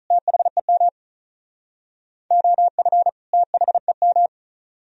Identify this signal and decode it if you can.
Morse code, heard as a keyed tone, about 43 wpm, 700 Hz. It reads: THEM OF THEM
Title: Speed-Racing - Sets of 2 Words 35wpm